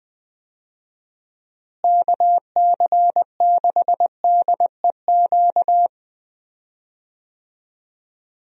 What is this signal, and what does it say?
Morse code, heard as a keyed tone, about 20 wpm, 700 Hz. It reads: KC6DEQ
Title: USA Call Signs - 6 Characters - Repeated 2x - 20wpm